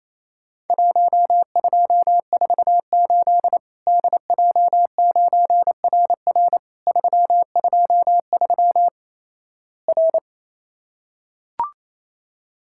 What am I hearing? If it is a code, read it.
Morse code, heard as a keyed tone, about 28 wpm, 700 Hz. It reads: 1248 DJ9RR 323 R
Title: Worked All Europe DX Contest - QTC Exchange - 28wpm